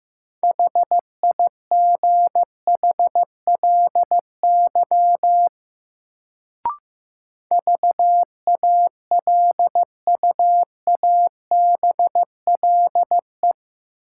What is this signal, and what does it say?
Morse code, heard as a keyed tone, about 15 wpm, 700 Hz. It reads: HIGHLY VALUABLE
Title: Sentences from Top 1100 Words - New Word Drill Alt - 15wpm